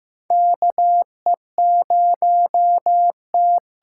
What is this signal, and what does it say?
Morse code, heard as a keyed tone, about 15 wpm, 700 Hz. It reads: KE0T
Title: USA Call Signs - 6 Characters - 8x Character Spacing - 15wpm